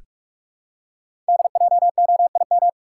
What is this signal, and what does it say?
Morse code, heard as a keyed tone, about 44 wpm, 700 Hz. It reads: DJOIM
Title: Code Groups - 5 Letters - 45wpm